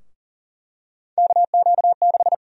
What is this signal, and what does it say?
Morse code, heard as a keyed tone, about 39 wpm, 700 Hz. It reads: KQ6
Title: USA Call Sign Prefixes - Repeated 2x - 40wpm